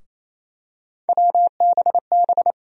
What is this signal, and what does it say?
Morse code, heard as a keyed tone, about 27 wpm, 700 Hz. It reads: WBB
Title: USA Call Sign Postfixes - 3 Letters 28wpm